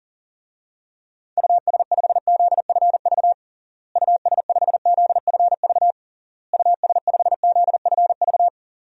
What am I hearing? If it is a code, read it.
Morse code, heard as a keyed tone, about 40 wpm, 700 Hz. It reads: US5ZFV US5ZFV US5ZFV
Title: DXCC-100 Call Signs - 6 Characters - Repeated 3x - 40wpm